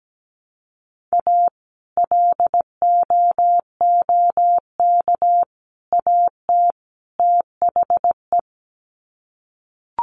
Morse code, heard as a keyed tone, about 17 wpm, 700 Hz. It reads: A LOOK AT THE
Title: Sets of 4 Words 17wpm